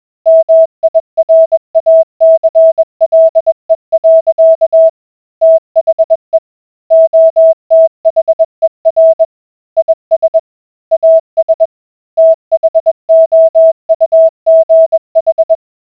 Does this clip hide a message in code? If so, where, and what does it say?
Yes: Morse code from 0.3 to 15.5 s, heard as a keyed tone, about 20 wpm, 650 Hz. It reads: MIRACLE. THE OTHER IS AS THOUGH